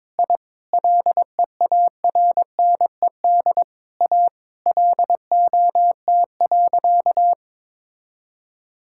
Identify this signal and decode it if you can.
Morse code, heard as a keyed tone, about 22 wpm, 700 Hz. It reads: I LEARNED A LOT.
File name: practice / 22wpm / Sentences with 3-Word Phrases - 22wpm